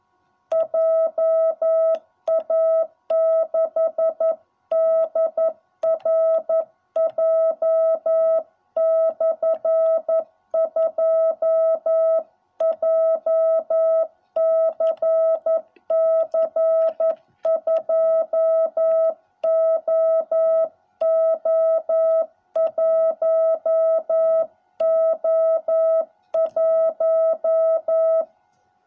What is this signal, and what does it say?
Morse code, heard as a keyed tone, about 11 wpm, 660 Hz. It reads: JA6DRJ/2 JCC 2OO1O1